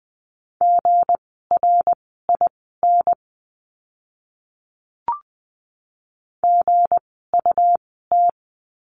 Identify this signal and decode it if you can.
Morse code, heard as a keyed tone, about 20 wpm, spaced as for 14 wpm, 700 Hz. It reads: GRIN GUT
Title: 160 Lesson 26f - Letter g - Words 20wpm